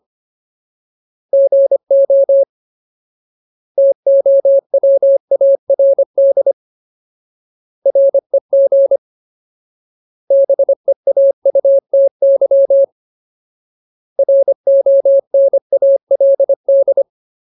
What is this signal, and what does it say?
Morse code, heard as a keyed tone, about 25 wpm, 550 Hz. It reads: GO TOWARD REG BEAUTY RONALD
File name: Ultimate Word List / 25WPM Ultimate